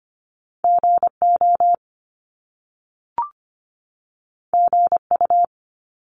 Morse code, heard as a keyed tone, about 25 wpm, 700 Hz. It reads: GO GU
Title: Top Two Letter Combinations in Words 25wpm